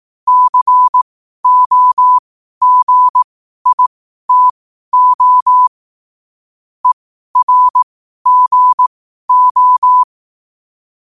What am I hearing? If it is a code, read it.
Morse code, heard as a keyed tone, about 18 wpm, spaced as for 12 wpm, 1000 Hz. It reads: COGITO ERGO
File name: qotdfiles / 12wpm